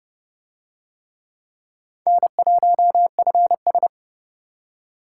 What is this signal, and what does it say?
Morse code, heard as a keyed tone, about 27 wpm, 700 Hz. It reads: N1FS